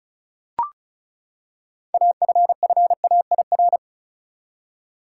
Morse code, heard as a keyed tone, about 34 wpm, 700 Hz. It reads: AFFAIR
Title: Six Letter Words - 35wpm